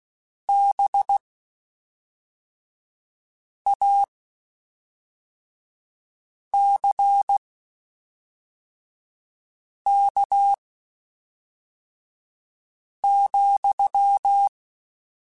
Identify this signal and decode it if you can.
Morse code, heard as a keyed tone, about 16 wpm, spaced as for 3 wpm, 800 Hz. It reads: BACK,
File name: (spaced) / QSOs Spaced-mp3s / kc7sac @5 sp